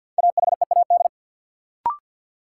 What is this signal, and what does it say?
Morse code, heard as a keyed tone, about 52 wpm, 700 Hz. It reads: AHEAD
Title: Sentences from Top 1000 Words - Review Part 6 - New Word Drill - 50wpm